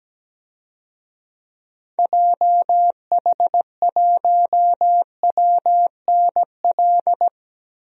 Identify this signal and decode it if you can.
Morse code, heard as a keyed tone, about 17 wpm, 700 Hz. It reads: JH1WNL
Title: DXCC-100 Call Signs - 6 Characters - Repeated 3x - 17wpm